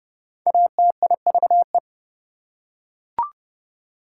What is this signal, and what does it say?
Morse code, heard as a keyed tone, about 29 wpm, 700 Hz. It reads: ATIVE
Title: Prefixes and Suffixes 30wpm